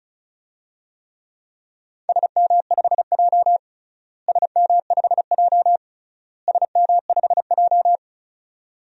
Morse code, heard as a keyed tone, about 36 wpm, 700 Hz. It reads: SM5J SM5J SM5J
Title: DXCC-100 Call Signs - 4 Characters - Repeated 3x - 35wpm